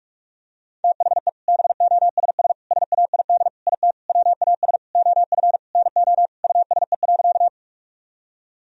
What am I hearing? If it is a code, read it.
Morse code, heard as a keyed tone, about 47 wpm, 700 Hz. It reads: THE BOSS SAID IT WAS OF NO USE.
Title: Sentences from Top 1000 Words - Review Part 4 - 45wpm